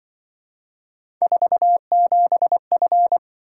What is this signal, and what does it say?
Morse code, heard as a keyed tone, about 24 wpm, 700 Hz. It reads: 47F
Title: Speed-Racing - Three Letter-Number Combinations - 20wpm